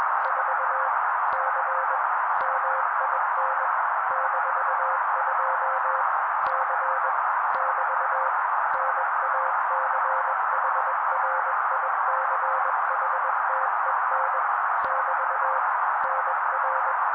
Morse code, heard as a keyed tone, about 21 wpm, 530 Hz. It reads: 4 C MIN -2 C <BT> NACHRICHTEN <BT> NR
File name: daily / 20wpm noise